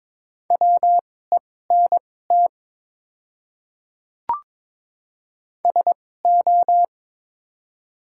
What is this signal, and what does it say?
Morse code, heard as a keyed tone, about 23 wpm, spaced as for 16 wpm, 700 Hz. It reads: WENT SO
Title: 2x Character Spacing - Top 200 Words - 22wpm